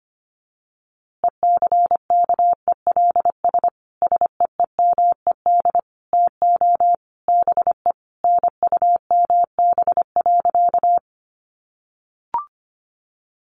Morse code, heard as a keyed tone, about 25 wpm, 700 Hz. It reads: ECKELS SEEMED TO BE NUMB.
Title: A Sound Of Thunder 25wpm